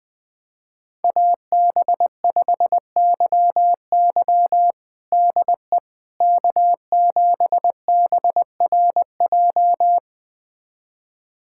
Morse code, heard as a keyed tone, about 21 wpm, 700 Hz. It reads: AB5YY DE K7BRJ
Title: USA Callsign DE Callsign - 5 Characters - 20wpm